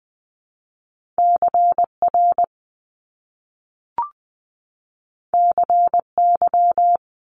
Morse code, heard as a keyed tone, about 20 wpm, 700 Hz. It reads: CR CY
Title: Top Two Letter Combinations in Words 20wpm